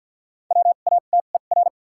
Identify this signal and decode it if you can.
Morse code, heard as a keyed tone, about 52 wpm, spaced as for 37 wpm, 700 Hz. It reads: WATER
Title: 2x Character Spacing - Top 200 Words - 50wpm